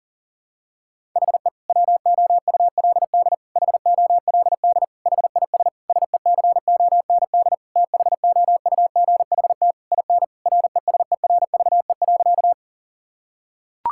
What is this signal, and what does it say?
Morse code, heard as a keyed tone, about 37 wpm, 700 Hz. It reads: HE WOULD HOLD HIS SECOND THOUGHT IN RESERVE.
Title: Sentences from Top 1000 Words - Review Part 6 - 40wpm